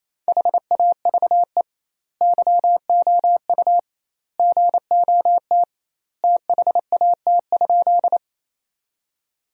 Morse code, heard as a keyed tone, about 28 wpm, 700 Hz. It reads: HAVE YOU GOT THAT?
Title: 2x Word-Spacing - Sentences from Top 300 Words - 28wpm